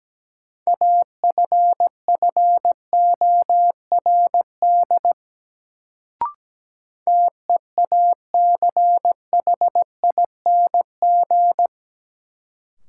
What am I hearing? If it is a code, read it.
Morse code, heard as a keyed tone, about 17 wpm, 700 Hz. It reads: AFFORD TEACHING
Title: Sentences from Top 1100 Words - New Word Drill Alt - 17wpm